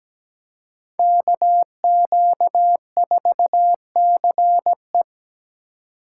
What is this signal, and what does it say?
Morse code, heard as a keyed tone, about 20 wpm, 700 Hz. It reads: KQ4CE
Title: Speed Racing - USA 2x2 Call Signs 17wpm